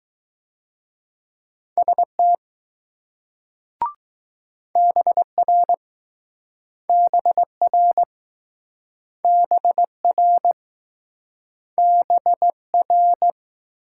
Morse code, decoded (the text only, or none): ST BR BR BR BR